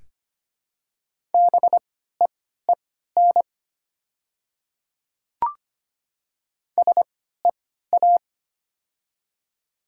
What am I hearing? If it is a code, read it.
Morse code, heard as a keyed tone, about 25 wpm, spaced as for 14 wpm, 690 Hz. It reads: BEEN SEA
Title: 3x Character Spacing - Top 300 Words - 25wpm